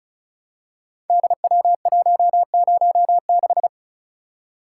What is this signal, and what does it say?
Morse code, heard as a keyed tone, about 36 wpm, 690 Hz. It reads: DW106
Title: Code Groups - 5 Letter:Numbers - 35wpm